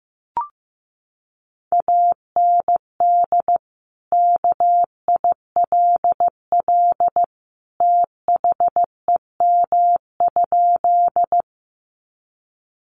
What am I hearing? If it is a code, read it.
Morse code, heard as a keyed tone, about 15 wpm, 700 Hz. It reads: AND KILL THEM?
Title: Sentences from Top 1400 Words 15wpm